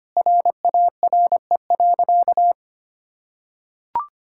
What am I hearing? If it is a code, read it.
Morse code, heard as a keyed tone, about 25 wpm, 700 Hz. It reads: RARE.
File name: practice / 25wpm / Sentences with 2-Word Phrases - 25wpm